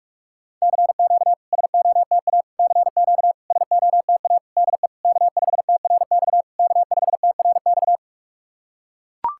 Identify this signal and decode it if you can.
Morse code, heard as a keyed tone, about 44 wpm, 690 Hz. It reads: CQ SOTA CQ SOTA DE K5TRX K5TRX
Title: SOTA QSO Practice 45wpm